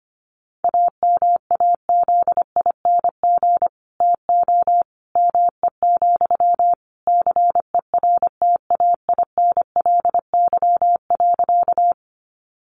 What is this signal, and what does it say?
Morse code, heard as a keyed tone, about 25 wpm, 700 Hz. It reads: AMAZING TO ME, CERTAINLY.
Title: Sentences from Top 1500 Words 25wpm